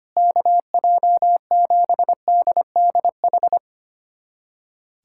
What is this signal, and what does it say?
Morse code, heard as a keyed tone, about 25 wpm, 700 Hz. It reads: KJ7DDH
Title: USA Callsign DE Callsign - 6 Characters - 25wpm